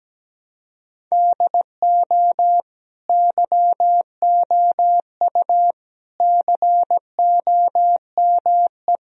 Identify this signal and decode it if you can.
Morse code, heard as a keyed tone, about 17 wpm, 700 Hz. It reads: DO YOU COME
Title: Sentences from Top 1000 Words - Review Part 1 - 17wpm